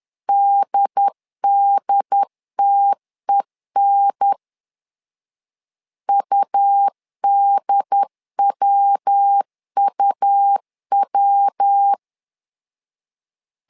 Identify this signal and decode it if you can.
Morse code, heard as a keyed tone, about 10 wpm, 800 Hz. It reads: DDTEN UDWUW
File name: LEZIONI / 3 MWRO / 3 MW 10wpm 1959